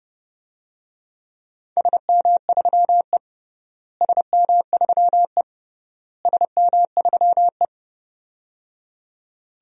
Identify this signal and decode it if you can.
Morse code, heard as a keyed tone, about 30 wpm, 700 Hz. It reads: SM3E SM3E SM3E
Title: DXCC-100 Call Signs - 4 Characters - Repeated 3x - 30wpm